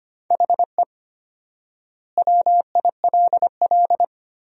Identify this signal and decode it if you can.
Morse code, heard as a keyed tone, about 25 wpm, 700 Hz. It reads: HE WILL
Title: 4x Word-Spacing - Sentences from Top 400 Words - 25wpm